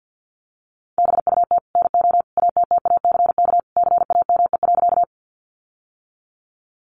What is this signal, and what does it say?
Morse code, heard as a keyed tone, about 48 wpm, 700 Hz. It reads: BUT NO ATTACK CAME.
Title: Sentences from Top 1500 Words 50wpm